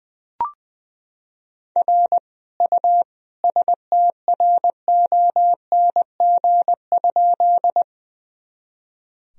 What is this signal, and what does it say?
Morse code, heard as a keyed tone, about 21 wpm, 700 Hz. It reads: R U STRONG?
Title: Sentences from Top 500 Words and Abbreviations 20wpm